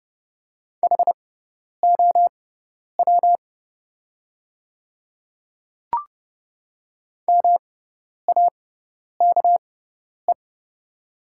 Morse code, heard as a keyed tone, about 30 wpm, spaced as for 10 wpm, 700 Hz. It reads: HOW MAKE
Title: 6x Character Spacing - Top 100 Words - 30wpm